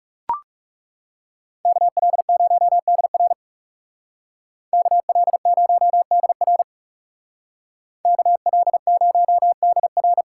Morse code, heard as a keyed tone, about 39 wpm, 700 Hz. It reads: KL0DR KL0DR KL0DR